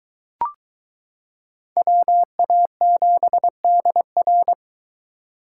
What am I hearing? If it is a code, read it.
Morse code, heard as a keyed tone, about 23 wpm, 700 Hz. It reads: WA7DR